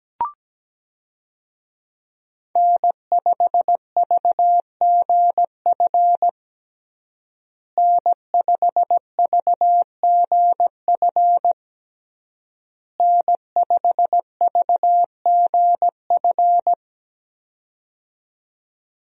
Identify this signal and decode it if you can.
Morse code, heard as a keyed tone, about 17 wpm, 700 Hz. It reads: N5VGF N5VGF N5VGF
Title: USA Call Signs - 5 Characters - Repeated 3x - 17wpm